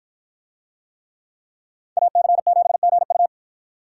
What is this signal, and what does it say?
Morse code, heard as a keyed tone, about 53 wpm, 700 Hz. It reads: AC7GU